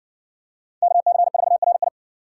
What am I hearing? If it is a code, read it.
Morse code, heard as a keyed tone, about 55 wpm, 700 Hz. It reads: KC4RI